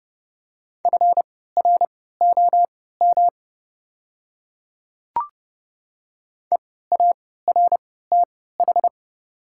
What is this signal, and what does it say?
Morse code, heard as a keyed tone, about 30 wpm, spaced as for 17 wpm, 700 Hz. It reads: FROM EARTH
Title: 3x Character Spacing - Top 300 Words - 30wpm